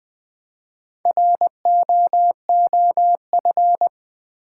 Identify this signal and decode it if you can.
Morse code, heard as a keyed tone, about 20 wpm, 690 Hz. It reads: ROOF
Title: Build A Word - 2 Letters At A Time - 4-Letter Words - 20wpm